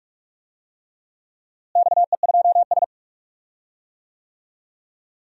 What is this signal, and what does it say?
Morse code, heard as a keyed tone, about 42 wpm, 700 Hz. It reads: XE2S
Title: DXCC-100 Call Signs - 4 Characters - Repeated 3x - 45wpm